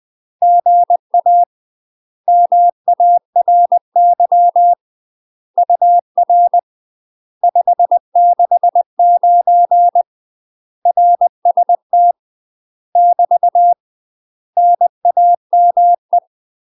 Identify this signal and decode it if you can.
Morse code, heard as a keyed tone, about 20 wpm, 700 Hz. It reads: GA MARY UR 569 RST <BT> NAME
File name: UK MASTER QSOs TRAINING / 20WPM / UK MASTER QSOs 0072